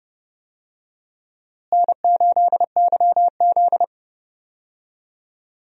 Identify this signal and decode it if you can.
Morse code, heard as a keyed tone, about 30 wpm, 700 Hz. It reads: N8YZ
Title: USA Call Signs - 4 Characters - Repeated 3x - 30wpm